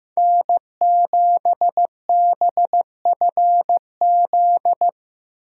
Morse code, heard as a keyed tone, about 15 wpm, 700 Hz. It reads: N7BFZ
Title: USA Call Signs - 5 Characters - Repeated 2x - 15wpm